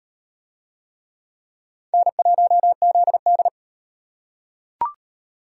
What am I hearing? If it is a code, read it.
Morse code, heard as a keyed tone, about 37 wpm, 700 Hz. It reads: N1ZD